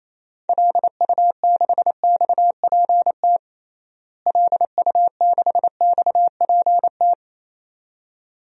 Morse code, heard as a keyed tone, about 28 wpm, 700 Hz. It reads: LU6XPT LU6XPT
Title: DXCC-100 Call Signs - 6 Characters - Repeated 2x - 28wpm